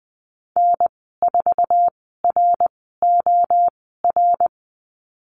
Morse code, heard as a keyed tone, about 20 wpm, spaced as for 15 wpm, 700 Hz. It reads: N4ROR
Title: 082 Lesson 14g - Number 2 - Call Signs 20wpm